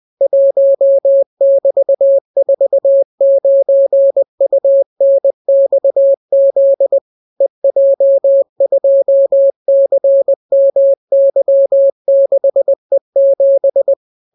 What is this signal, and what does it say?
Morse code, heard as a keyed tone, about 20 wpm, 550 Hz. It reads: 1<BT>49UNXZ EJ2CMY6E7